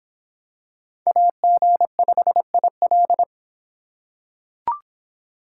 Morse code, heard as a keyed tone, about 25 wpm, 700 Hz. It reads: AG5IL